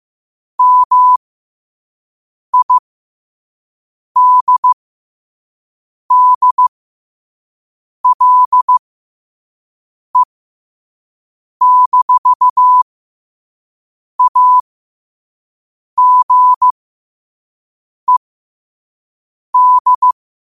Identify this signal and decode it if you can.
Morse code, heard as a keyed tone, about 15 wpm, spaced as for 5 wpm, 1000 Hz. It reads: MIDDLE-AGED